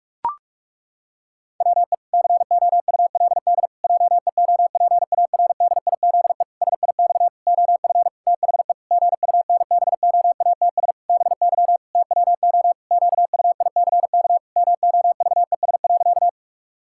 Morse code, heard as a keyed tone, about 46 wpm, 700 Hz. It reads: WE COULD JEOPARDIZE SIX OF THE GUNBOATS BY TWO QUICK MOVES.
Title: Pangram Challenge v2 - 45wpm